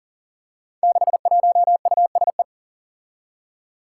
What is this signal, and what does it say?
Morse code, heard as a keyed tone, about 39 wpm, 700 Hz. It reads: 61USE
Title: Five Challenging Letter-Number Combinations 40wpm